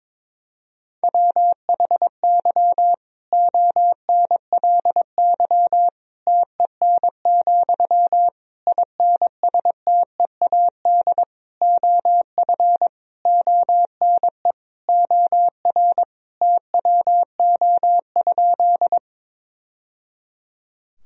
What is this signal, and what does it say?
Morse code, heard as a keyed tone, about 22 wpm, 700 Hz. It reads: WHY ONLY TEN, INSTEAD OF ONE OR TWO?
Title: Sentences from Top 1000 Words - Review Part 4 - 22wpm